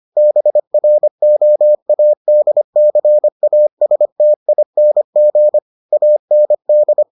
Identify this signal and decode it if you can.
Morse code, heard as a keyed tone, about 25 wpm, 600 Hz. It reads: BROADCASTING AND